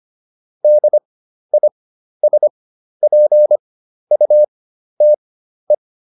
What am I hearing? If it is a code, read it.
Morse code, heard as a keyed tone, about 25 wpm, spaced as for 12 wpm, 600 Hz. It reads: DISPUTE